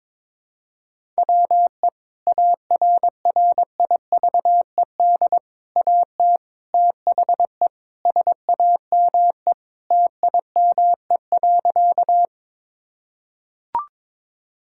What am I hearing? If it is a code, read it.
Morse code, heard as a keyed tone, about 21 wpm, 700 Hz. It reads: WE ARRIVED AT THE SAME TIME.
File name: practice / 22wpm / Sentences with 3-Word Phrases - 22wpm